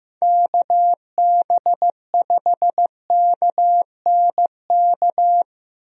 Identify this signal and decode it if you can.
Morse code, heard as a keyed tone, about 15 wpm, 700 Hz. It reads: KB5KNK
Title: USA Call Signs - 6 Characters - 15wpm